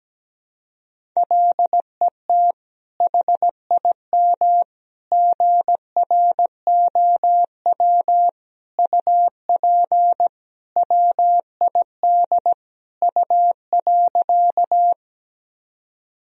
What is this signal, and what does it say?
Morse code, heard as a keyed tone, about 17 wpm, 700 Hz. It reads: LET HIM GROW UP WID U.
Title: Sentences from Top 300 Words and Abbreviations 17wpm